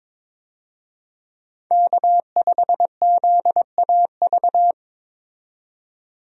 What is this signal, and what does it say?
Morse code, heard as a keyed tone, about 22 wpm, 700 Hz. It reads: K5ZAV
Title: USA Call Signs - 5 Characters - 22wpm